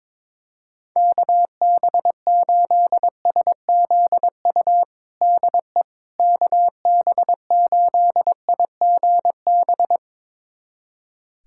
Morse code, heard as a keyed tone, about 22 wpm, 700 Hz. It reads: KB8SZU DE KB8IGB
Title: USA Callsign DE Callsign - 6 Characters - 22wpm